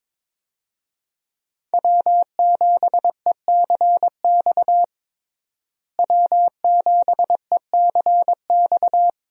Morse code, heard as a keyed tone, about 22 wpm, 700 Hz. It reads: W7ECX W7ECX
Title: USA Call Signs - 5 Characters - Repeated 2x - 22wpm